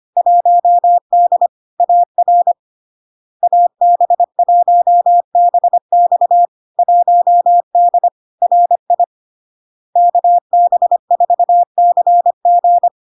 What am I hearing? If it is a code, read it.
Morse code, heard as a keyed tone, about 25 wpm, 700 Hz. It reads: 1D AR AB1BX 1D RI KB4CG
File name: LARK-FD-2022-Log-in-CW-25wpm-700-Hz